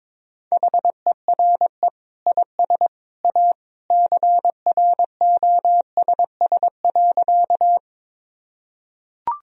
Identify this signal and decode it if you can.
Morse code, heard as a keyed tone, about 22 wpm, 700 Hz. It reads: HERE IS A CROSS.
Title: Sentences from Top 1000 Words - Review Part 1 - 22wpm